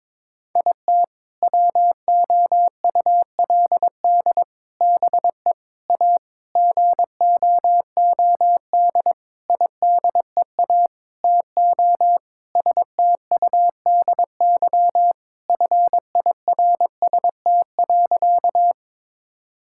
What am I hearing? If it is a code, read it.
Morse code, heard as a keyed tone, about 22 wpm, 700 Hz. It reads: IT WOULD BE A GOOD IDEA TO STUDY FIRST.
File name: Sentences with 7-Word Phrases - 22wpm